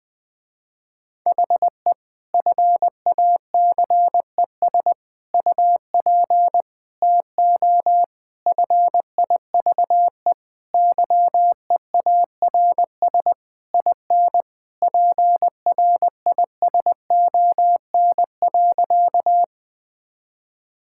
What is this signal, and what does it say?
Morse code, heard as a keyed tone, about 20 wpm, 700 Hz. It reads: HE FACES UP TO FIVE YEARS IN PRISON.
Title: Sentences with 6-Word Phrases - Spoken Beginning - 20wpm